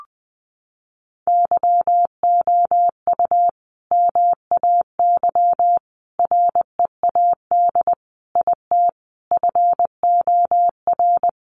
Morse code, heard as a keyed tone, about 20 wpm, 700 Hz. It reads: YOU MAY READ IT FOR
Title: Sentences from Top 600 Words 20wpm